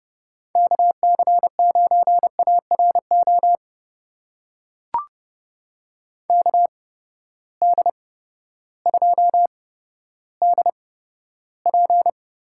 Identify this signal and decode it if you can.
Morse code, heard as a keyed tone, about 30 wpm, spaced as for 14 wpm, 700 Hz. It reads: KC9ARO KD2DP
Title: USA Call Signs - 6 Characters - 8x Character Spacing - 30wpm